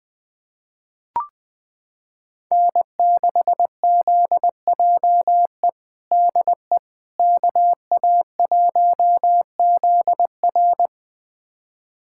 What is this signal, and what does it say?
Morse code, heard as a keyed tone, about 21 wpm, 700 Hz. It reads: N6ZJE DE KA1ZR
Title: USA Callsign DE Callsign - 5 Characters - 20wpm